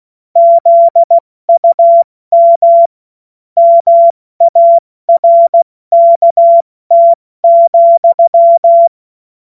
Morse code, heard as a keyed tone, about 16 wpm, 680 Hz. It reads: ZUM MARKT,
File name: Witz-16-September-2025-70-F80